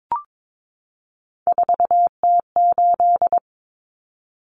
Morse code, heard as a keyed tone, about 22 wpm, 700 Hz. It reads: 4T8
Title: Common Cut Numbers - 3-Digit 22wpm